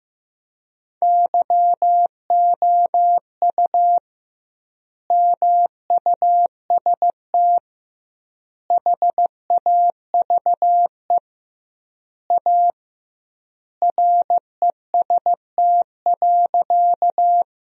Morse code, heard as a keyed tone, about 16 wpm, 700 Hz. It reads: YOU MUST HAVE A REST.
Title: 2x Word-Spacing - Sentences from Top 500 Words - 15wpm